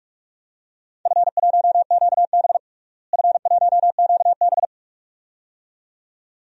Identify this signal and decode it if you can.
Morse code, heard as a keyed tone, about 44 wpm, 700 Hz. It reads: F1QB F1QB
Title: DXCC-100 Call Signs - 4 Characters - Repeated 2x - 45wpm